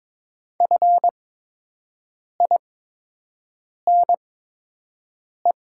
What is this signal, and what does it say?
Morse code, heard as a keyed tone, about 22 wpm, spaced as for 6 wpm, 700 Hz. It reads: FINE
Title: 8x Character Spacing - Top 500 Words - 22wpm